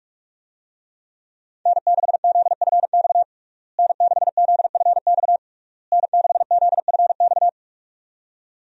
Morse code, heard as a keyed tone, about 46 wpm, 700 Hz. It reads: N6ZFX N6ZFX N6ZFX
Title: USA Call Signs - 5 Characters - Repeated 3x - 45wpm